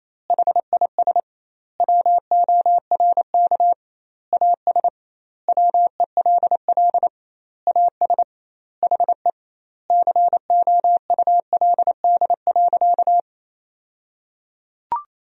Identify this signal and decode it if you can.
Morse code, heard as a keyed tone, about 28 wpm, 700 Hz. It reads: HIS WORK AS WELL AS HE COULD.
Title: 2x Word-Spacing - Sentences from Top 500 Words - 28wpm